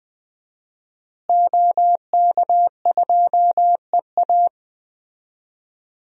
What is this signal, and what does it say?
Morse code, heard as a keyed tone, about 20 wpm, 700 Hz. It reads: OK2EA
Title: DXCC-100 Call Signs - 5 Characters - Repeated 2x - 20wpm